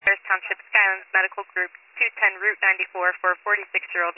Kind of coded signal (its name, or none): Emergency Alert System (SAME) tones